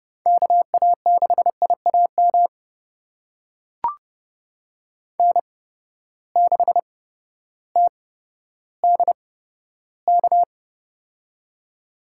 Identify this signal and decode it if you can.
Morse code, heard as a keyed tone, about 32 wpm, spaced as for 14 wpm, 700 Hz. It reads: KA6IAM N6TDK
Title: USA Call Signs - 8x Character Spacing - 30wpm